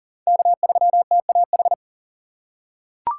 Morse code, heard as a keyed tone, about 40 wpm, 700 Hz. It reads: K3TAH
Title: USA Call Signs - 5 Characters - 2x Character Spacing - 40wpm